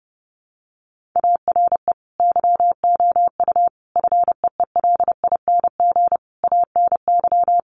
Morse code, heard as a keyed tone, about 30 wpm, 700 Hz. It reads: ARE YOU FEELING ANY
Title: Sentences from Top 1200 Words 30wpm